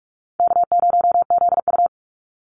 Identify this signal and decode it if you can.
Morse code, heard as a keyed tone, about 44 wpm, 700 Hz. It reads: K0ZU
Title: Easy Call Signs - 45wpm